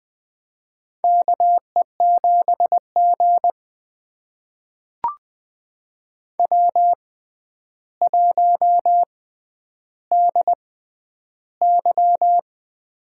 Morse code, heard as a keyed tone, about 20 wpm, spaced as for 10 wpm, 700 Hz. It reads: KE7G W1DY
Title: USA Call Signs - 4 Characters - 6x Character Spacing - 20wpm